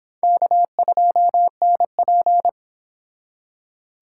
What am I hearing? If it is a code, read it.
Morse code, heard as a keyed tone, about 26 wpm, 700 Hz. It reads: K2NP